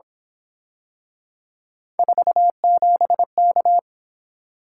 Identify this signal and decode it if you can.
Morse code, heard as a keyed tone, about 27 wpm, 700 Hz. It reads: 47K
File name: Speed-Racing - Three Letter-Number Combinations - 22wpm